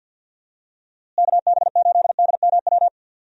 Morse code, heard as a keyed tone, about 51 wpm, 700 Hz. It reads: KB8DMW
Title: USA Call Signs - 6 Characters - Repeated 2x - 50wpm